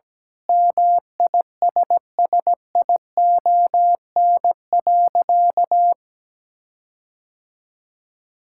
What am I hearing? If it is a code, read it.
Morse code, heard as a keyed tone, about 17 wpm, 700 Hz. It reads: MISSION.
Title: Sentences from Top 1600 Words 17wpm